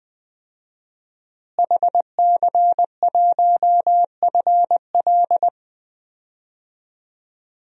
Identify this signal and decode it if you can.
Morse code, heard as a keyed tone, about 20 wpm, 700 Hz. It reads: HC1FL
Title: DXCC-100 Call Signs - 5 Characters - Repeated 2x - 20wpm